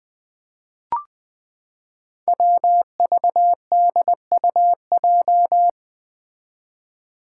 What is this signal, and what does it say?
Morse code, heard as a keyed tone, about 20 wpm, 700 Hz. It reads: WVDUJ
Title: Five Challenging Letter-Number Combinations 20wpm